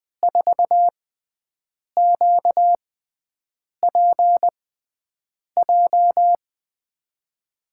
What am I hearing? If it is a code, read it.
Morse code, heard as a keyed tone, about 21 wpm, spaced as for 7 wpm, 700 Hz. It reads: 4QPJ
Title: USA Call Signs - 6 Characters - 6x Character Spacing - 20wpm